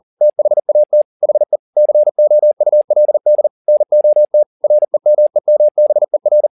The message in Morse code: THAT HE COULD NOT REMEMBER

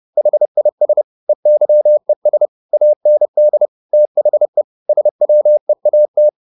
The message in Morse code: HIS EYES AND THE SWEAT

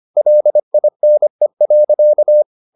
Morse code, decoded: LINE.

